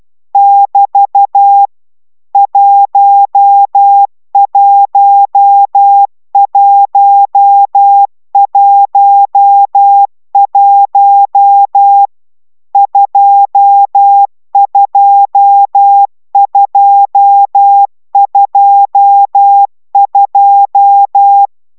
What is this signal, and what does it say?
Morse code, heard as a keyed tone, about 12 wpm, 800 Hz. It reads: <BT> 11111 22222
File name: taviro tanulas / cw 69 22 10